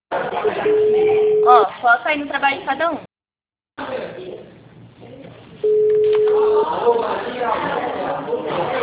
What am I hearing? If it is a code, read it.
Ringback tone (Europe, 425 Hz), with speech.